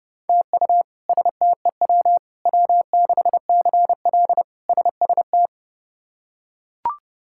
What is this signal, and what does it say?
Morse code, heard as a keyed tone, about 32 wpm, 700 Hz. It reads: TU STEW W6CL SST
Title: K1USN SST QSO Practice - 30wpm